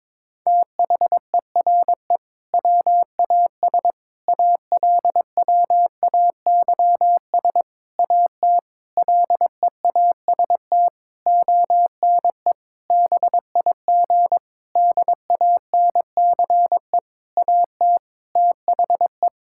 Morse code, heard as a keyed tone, about 22 wpm, 700 Hz. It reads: THERE WAS ALWAYS AT LEAST ONE BIG DANCE AT THE
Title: Sentences from Top 1600 Words 22wpm